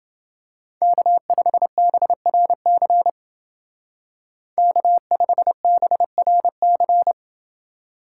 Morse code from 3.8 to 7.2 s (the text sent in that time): K5BRC